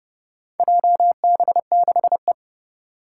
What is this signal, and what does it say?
Morse code, heard as a keyed tone, about 30 wpm, 700 Hz. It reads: JB6E
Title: Four Challenging Letter-Number Combinations 30wpm